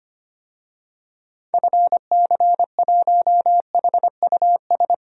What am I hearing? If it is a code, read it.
Morse code, heard as a keyed tone, about 25 wpm, 700 Hz. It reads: FC1HUS
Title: DXCC-100 Call Signs - 6 Characters - 25wpm